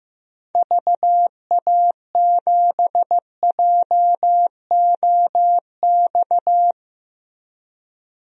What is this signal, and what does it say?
Morse code, heard as a keyed tone, about 15 wpm, 700 Hz. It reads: VA7JOX
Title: DXCC-100 Call Signs - 6 Characters - 15wpm